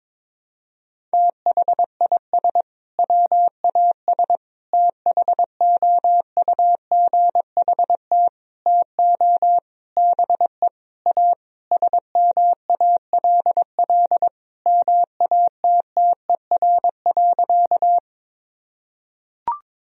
Morse code, heard as a keyed tone, about 22 wpm, 700 Hz. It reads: THIS WAS THOUGHT TO BE A SMALL MATTER.
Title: Sentences from Top 1000 Words - Review Part 1 - 22wpm